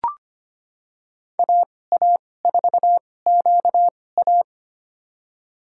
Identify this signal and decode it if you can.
Morse code, heard as a keyed tone, about 25 wpm, spaced as for 18 wpm, 700 Hz. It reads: AA4QA
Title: USA Call Signs - 2x Character Spacing - 25wpm